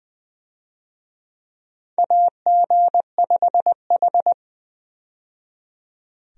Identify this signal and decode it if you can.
Morse code, heard as a keyed tone, about 20 wpm, 700 Hz. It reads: AG5H
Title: USA Call Signs - 4 Characters - 20wpm